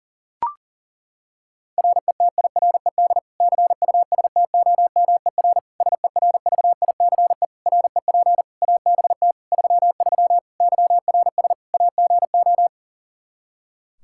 Morse code, heard as a keyed tone, about 37 wpm, 700 Hz. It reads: RETIRED CUSTOMER SERVICE REP ABT 33 YRS AGO
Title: Practice QSO - Exchange 3 - More Info - 40wpm